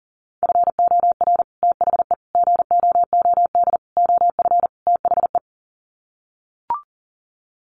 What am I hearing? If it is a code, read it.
Morse code, heard as a keyed tone, about 40 wpm, 700 Hz. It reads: FOR THE GOOD OF THE
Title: Sets of 5 Words Encore2 40wpm